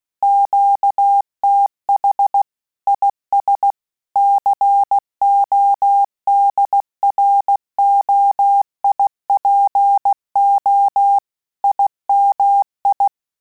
Morse code, heard as a keyed tone, about 16 wpm, 800 Hz. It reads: QTH IS CODROIPO IMI
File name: Telegraphy-group / QSO CW 15 PPM